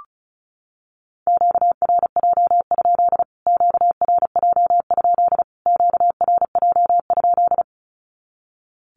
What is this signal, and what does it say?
Morse code, heard as a keyed tone, about 34 wpm, 700 Hz. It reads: QRJ? QRJ? QRJ?
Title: Q-Codes Repeated 3x 35wpm